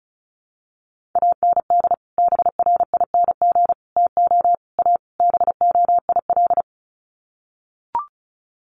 Morse code, heard as a keyed tone, about 34 wpm, 700 Hz. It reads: AND BRING TO A BOIL
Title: Sets of 5 Words Encore 35wpm